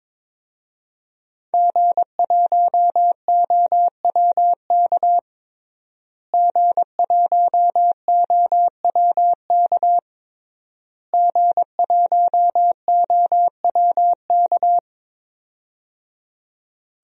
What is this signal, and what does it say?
Morse code, heard as a keyed tone, about 22 wpm, 700 Hz. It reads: G1OWK G1OWK G1OWK
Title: DXCC-100 Call Signs - 5 Characters - Repeated 3x - 22wpm